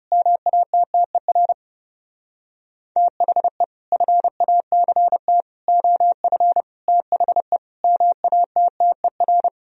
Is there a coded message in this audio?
Yes: Morse code from 0.1 to 9.5 s, heard as a keyed tone, about 38 wpm, 700 Hz. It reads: MATTER THE FACT OF THE MATTER